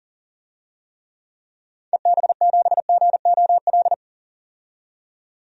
Morse code, heard as a keyed tone, about 40 wpm, 700 Hz. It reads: EB7GOL
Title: DXCC-100 Call Signs - 6 Characters - 40wpm